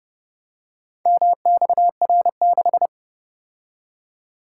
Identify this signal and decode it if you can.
Morse code, heard as a keyed tone, about 30 wpm, 700 Hz. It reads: MXR6